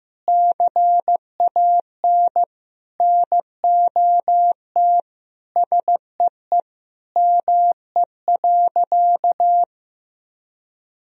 Morse code, heard as a keyed tone, about 15 wpm, 700 Hz. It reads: CAN NOT SEE ME.
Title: Sentences from Top 1000 Words - Review Part 3 - 15wpm